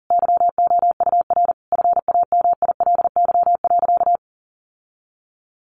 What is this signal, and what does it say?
Morse code, heard as a keyed tone, about 40 wpm, 700 Hz. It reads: YOUR FAMILY.
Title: Sentences from Top 1400 Words 40wpm